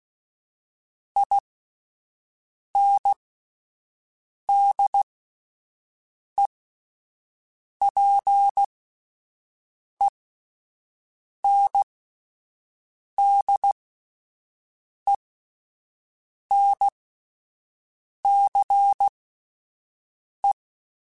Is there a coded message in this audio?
Yes: Morse code from 1.2 to 20.5 s, heard as a keyed tone, about 16 wpm, spaced as for 5 wpm, 800 Hz. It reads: INDEPENDENCE